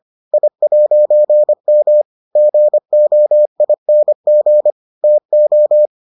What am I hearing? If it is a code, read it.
Morse code, heard as a keyed tone, about 25 wpm, 600 Hz. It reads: I'M GOING TO